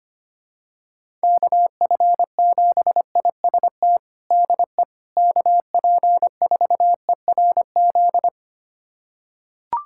Morse code, heard as a keyed tone, about 25 wpm, 700 Hz. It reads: KF7IST DE KP4ERZ
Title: USA Callsign DE Callsign - 6 Characters - 25wpm